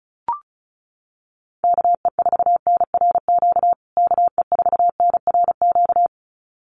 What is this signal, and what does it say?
Morse code, heard as a keyed tone, about 34 wpm, 700 Hz. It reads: KE4NRQ KE4NRQ
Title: USA Call Signs Repeated 2x 35wpm